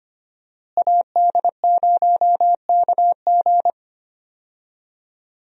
Morse code, heard as a keyed tone, about 25 wpm, 700 Hz. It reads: AD0KG
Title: USA Call Signs - 5 Characters - Repeated 3x - 25wpm